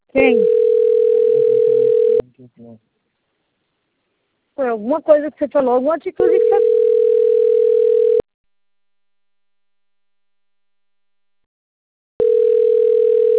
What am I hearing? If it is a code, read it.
Ringback tone (North America), 440 + 480 Hz, with speech.
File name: rg-1600-unknown